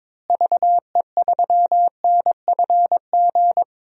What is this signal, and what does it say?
Morse code, heard as a keyed tone, about 22 wpm, 700 Hz. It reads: VE3NFG
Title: DXCC-100 Call Signs - 6 Characters - Repeated 3x - 22wpm